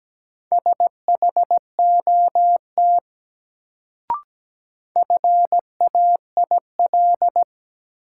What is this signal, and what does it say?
Morse code, heard as a keyed tone, about 17 wpm, 700 Hz. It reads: SHOT FAIL
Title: Sentences from Top 1100 Words - New Word Drill - 17wpm